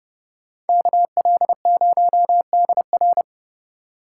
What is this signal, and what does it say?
Morse code, heard as a keyed tone, about 39 wpm, 700 Hz. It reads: KL0DR